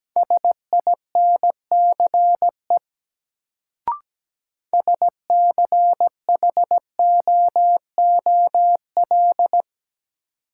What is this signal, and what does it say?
Morse code, heard as a keyed tone, about 17 wpm, 700 Hz. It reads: SINCE SCHOOL
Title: Sentences from Top 300 Words - New Word Drill - 17wpm